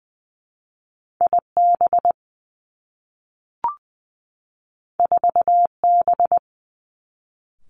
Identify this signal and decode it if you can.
Morse code, heard as a keyed tone, about 20 wpm, 700 Hz. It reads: I B 4 B
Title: Two Challenging Letter-Number Combinations 20wpm